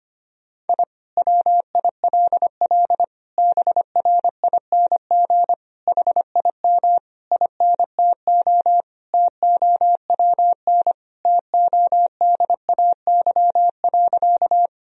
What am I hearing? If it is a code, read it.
Morse code, heard as a keyed tone, about 25 wpm, 700 Hz. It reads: I WILL BRING HIM INTO TOWN TODAY.
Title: Sentences from Top 1000 Words - Review Part 3 - 25wpm